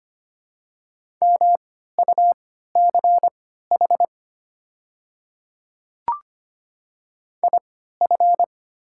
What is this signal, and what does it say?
Morse code, heard as a keyed tone, about 25 wpm, spaced as for 14 wpm, 700 Hz. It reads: MUCH IF
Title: 3x Character Spacing - Top 200 Words - 25wpm